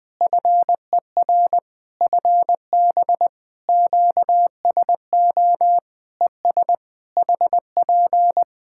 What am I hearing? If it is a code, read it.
Morse code, heard as a keyed tone, about 20 wpm, 700 Hz. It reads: FER FB QSO ES HP
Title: Practice QSO - Exchange 4 - Ending - 20wpm